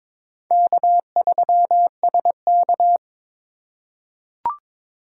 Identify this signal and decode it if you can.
Morse code, heard as a keyed tone, about 22 wpm, 700 Hz. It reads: K3SK
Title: USA Call Signs - 4 Characters - 3x Character Spacing - 22wpm